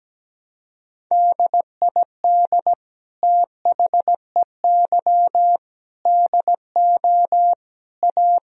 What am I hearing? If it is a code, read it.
Morse code, heard as a keyed tone, about 17 wpm, 700 Hz. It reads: DID THEY DO A
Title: Sentences from Top 1000 Words - Review Part 3 - 17wpm